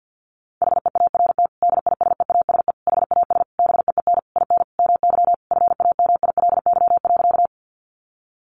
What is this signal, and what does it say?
Morse code, heard as a keyed tone, about 50 wpm, 700 Hz. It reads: HEART DISEASE HAS BEEN IN MY FAMILY.
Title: Sentences from Top 1200 Words 50wpm